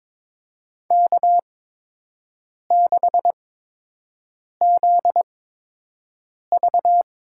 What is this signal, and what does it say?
Morse code, heard as a keyed tone, about 22 wpm, spaced as for 6 wpm, 700 Hz. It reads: K6ZV
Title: USA Call Signs - 4 Characters - 8x Character Spacing - 22wpm